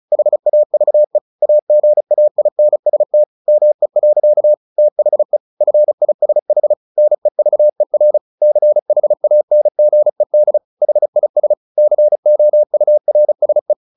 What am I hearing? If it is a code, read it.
Morse code, heard as a keyed tone, about 34 wpm, 600 Hz. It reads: HAVE AGAINST ME. THE FISH NEVER CHANGED HIS COURSE
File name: the old man and the sea / the old man and the sea 1 35wpm